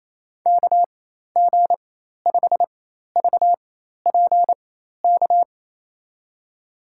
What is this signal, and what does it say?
Morse code, heard as a keyed tone, about 27 wpm, spaced as for 13 wpm, 700 Hz. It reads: KG5VPK
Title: USA Call Signs - 6 Characters - 4x Character Spacing - 28wpm